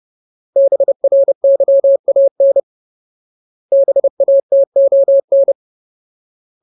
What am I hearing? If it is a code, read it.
Morse code, heard as a keyed tone, about 30 wpm, 550 Hz. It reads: BRYAN BATON
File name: Ultimate Word List / 30WPM Ultimate